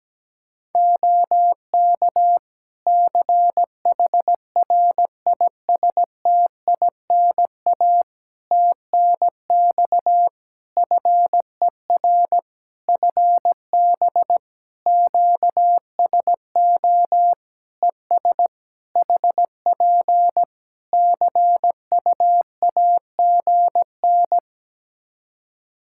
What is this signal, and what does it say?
Morse code, heard as a keyed tone, about 17 wpm, 700 Hz. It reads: OK CHRISTINA TNX FER FB QSO ES HP CUAGN
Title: Practice QSO - Exchange 4 - Ending - 17wpm